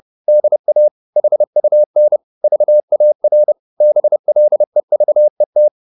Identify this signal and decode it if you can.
Morse code, heard as a keyed tone, about 30 wpm, 600 Hz. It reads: DA HUN VAR BLEVET
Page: Ferie 30 wpm